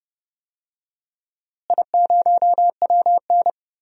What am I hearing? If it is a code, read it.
Morse code, heard as a keyed tone, about 30 wpm, 700 Hz. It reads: I0WN
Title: DXCC-100 Call Signs - 4 Characters - Repeated 3x - 30wpm